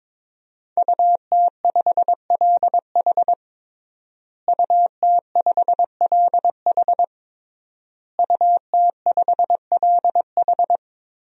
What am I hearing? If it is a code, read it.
Morse code, heard as a keyed tone, about 22 wpm, 700 Hz. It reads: UT5LH UT5LH UT5LH
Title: DXCC-100 Call Signs - 5 Characters - Repeated 3x - 22wpm